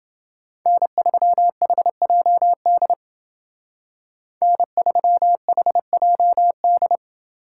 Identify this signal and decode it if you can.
Morse code, heard as a keyed tone, about 28 wpm, 700 Hz. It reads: N3HJD N3HJD